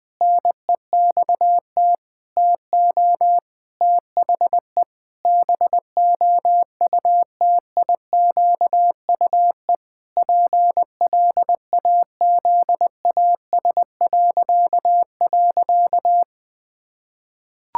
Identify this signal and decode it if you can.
Morse code, heard as a keyed tone, about 21 wpm, 700 Hz. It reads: NEXT TO THE BOUTIQUE PLAZAS..
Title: Pangram Challenge v2 - 20wpm